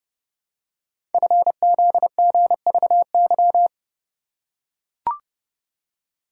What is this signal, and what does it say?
Morse code, heard as a keyed tone, about 30 wpm, 700 Hz. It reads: FZGVY
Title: Code Groups - 5 Letters - 30wpm